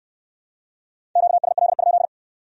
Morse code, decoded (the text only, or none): QSL?